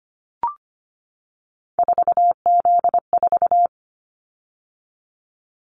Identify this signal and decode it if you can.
Morse code, heard as a keyed tone, about 25 wpm, 700 Hz. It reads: 4Z4
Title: Callsign Prefixes for Top 50 DXCC - 25wpm